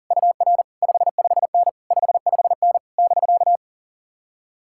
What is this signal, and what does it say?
Morse code, heard as a keyed tone, about 49 wpm, 700 Hz. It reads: UR 55N 55N <BK>
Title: Speed-Racing Common QSO Phrases Encore - 40wpm